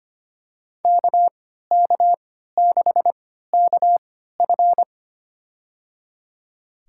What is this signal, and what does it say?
Morse code, heard as a keyed tone, about 25 wpm, spaced as for 14 wpm, 700 Hz. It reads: KK6KF
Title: USA Call Signs - 5 Characters - 3x Character Spacing - 25wpm